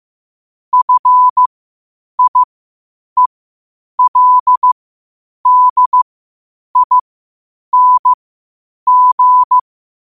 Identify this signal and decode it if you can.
Morse code, heard as a keyed tone, about 15 wpm, spaced as for 8 wpm, 1000 Hz. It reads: FIELDING
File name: qotdfiles / 8wpm